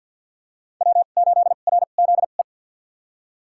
Morse code, heard as a keyed tone, about 51 wpm, spaced as for 36 wpm, 700 Hz. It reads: W8RZE
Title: USA Call Signs - 2x Character Spacing - 50wpm